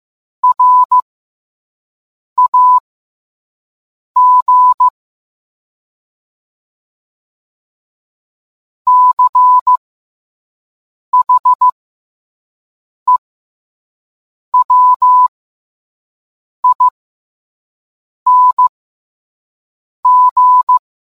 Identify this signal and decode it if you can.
Morse code, heard as a keyed tone, about 15 wpm, spaced as for 5 wpm, 1000 Hz. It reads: RAG CHEWING